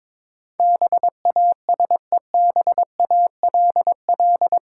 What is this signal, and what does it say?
Morse code, heard as a keyed tone, about 22 wpm, 700 Hz. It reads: BASEBALL
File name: Difficult Words with Hints - 22wpm